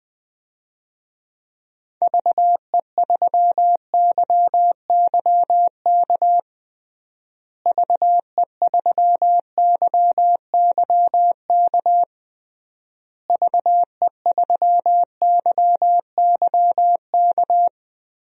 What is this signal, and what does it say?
Morse code, heard as a keyed tone, about 20 wpm, 700 Hz. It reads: VE3YYK VE3YYK VE3YYK
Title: DXCC-100 Call Signs - 6 Characters - Repeated 3x - 20wpm